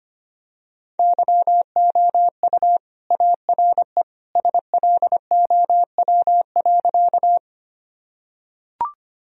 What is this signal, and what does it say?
Morse code, heard as a keyed tone, about 25 wpm, 700 Hz. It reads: YOU ARE SLOW.
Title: Sentences from Top 1000 Words - Review Part 2 - 25wpm